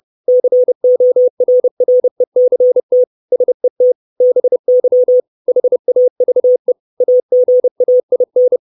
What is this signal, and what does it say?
Morse code, heard as a keyed tone, about 30 wpm, 500 Hz. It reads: CORRECT SET BY HAVE AGAIN
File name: 632 Words 30WPM